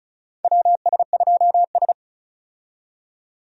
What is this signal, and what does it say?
Morse code, heard as a keyed tone, about 36 wpm, 700 Hz. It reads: WS2S
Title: USA Call Signs - 4 Characters - 35wpm